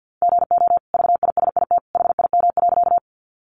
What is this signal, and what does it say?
Morse code, heard as a keyed tone, about 50 wpm, 700 Hz. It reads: GO VISIT HIM.
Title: Sentences from Top 900 Words 50wpm